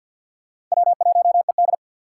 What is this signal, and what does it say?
Morse code, heard as a keyed tone, about 48 wpm, 700 Hz. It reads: W1ED
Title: USA Call Signs - 6x Character Spacing - 50wpm